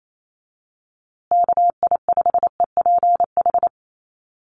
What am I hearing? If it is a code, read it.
Morse code, heard as a keyed tone, about 28 wpm, 700 Hz. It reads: KI5EPH
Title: USA Call Signs Repeated 2x 28wpm